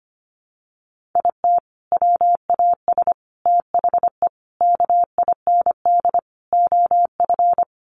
Morse code, heard as a keyed tone, about 25 wpm, 700 Hz. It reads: IT WAS THE KIND OF
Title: Sets of 5 Words Encore2 25wpm